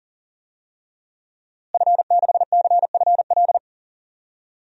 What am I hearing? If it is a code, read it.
Morse code, heard as a keyed tone, about 39 wpm, 700 Hz. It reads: F6CFL
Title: DXCC-100 Call Signs - 5 Characters - Repeated 2x - 40wpm